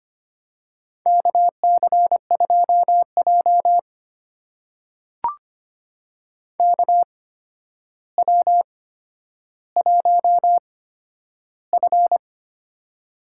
Morse code, heard as a keyed tone, about 25 wpm, spaced as for 11 wpm, 700 Hz. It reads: KC2J KW1F
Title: USA Call Signs - 4 Characters - 8x Character Spacing - 25wpm